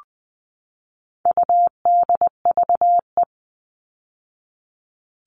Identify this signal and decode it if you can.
Morse code, heard as a keyed tone, about 20 wpm, 700 Hz. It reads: UDVE
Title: All Cut Numbers - 4-Digit 20wpm